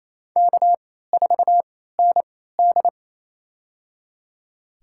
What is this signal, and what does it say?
Morse code, heard as a keyed tone, about 28 wpm, spaced as for 16 wpm, 700 Hz. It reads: K4ND
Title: USA Call Signs - 4 Characters - 3x Character Spacing - 28wpm